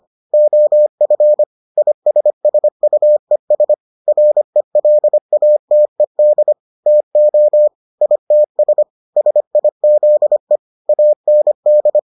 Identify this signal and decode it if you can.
Morse code, heard as a keyed tone, about 25 wpm, 600 Hz. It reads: OF ISSUES RELATED TO ITS SIZE AND